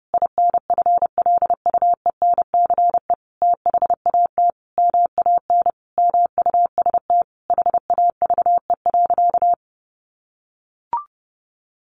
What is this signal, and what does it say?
Morse code, heard as a keyed tone, about 30 wpm, 700 Hz. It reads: INFLUENCE THAT MAN MUST HAVE.
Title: Sentences from Top 1100 Words 30wpm